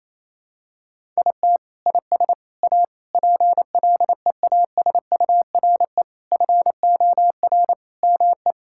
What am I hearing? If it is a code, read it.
Morse code, heard as a keyed tone, about 28 wpm, 700 Hz. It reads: IT IS A PLEASURE FOR ME
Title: Sentences from Top 1600 Words 28wpm